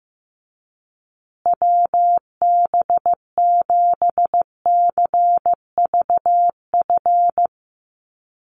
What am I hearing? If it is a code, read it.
Morse code, heard as a keyed tone, about 15 wpm, 700 Hz. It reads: WB7CVF
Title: USA Call Signs Repeated 2x 15wpm